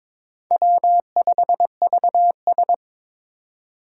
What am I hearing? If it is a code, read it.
Morse code, heard as a keyed tone, about 21 wpm, 700 Hz. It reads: W5VS
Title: USA Call Signs - 4 Characters - 2x Character Spacing - 22wpm